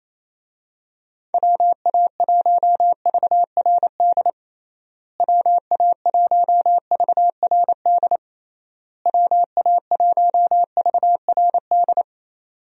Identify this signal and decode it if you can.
Morse code, heard as a keyed tone, about 27 wpm, 700 Hz. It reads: WA1VRD WA1VRD WA1VRD
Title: USA Call Signs - 6 Characters - Repeated 3x - 28wpm